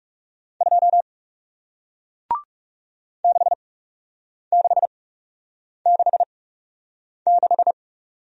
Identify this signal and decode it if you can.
Morse code, heard as a keyed tone, about 40 wpm, spaced as for 8 wpm, 700 Hz. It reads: 2 6666